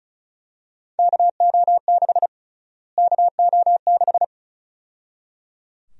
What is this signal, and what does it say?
Morse code, heard as a keyed tone, about 34 wpm, 700 Hz. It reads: KO6 KO6
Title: USA Call Sign Prefixes - Repeated 2x - 35wpm